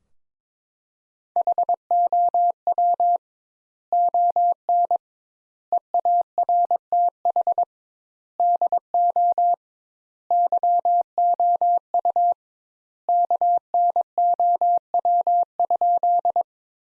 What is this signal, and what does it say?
Morse code, heard as a keyed tone, about 21 wpm, 700 Hz. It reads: HOW ON EARTH DO YOU KNOW?
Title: Sentences Top 200 Words - 22wpm